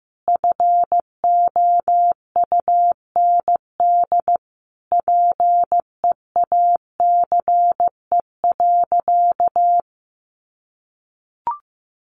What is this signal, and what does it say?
Morse code, heard as a keyed tone, about 14 wpm, 700 Hz. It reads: FOUND PEACE.
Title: Sentences from Top 1500 Words 15wpm